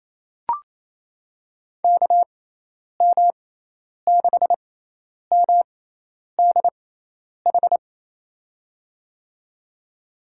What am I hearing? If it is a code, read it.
Morse code, heard as a keyed tone, about 28 wpm, spaced as for 10 wpm, 700 Hz. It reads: KM6MDH
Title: USA Call Signs - 6 Characters - 6x Character Spacing - 28wpm